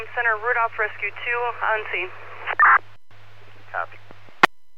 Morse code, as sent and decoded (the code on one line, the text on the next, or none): none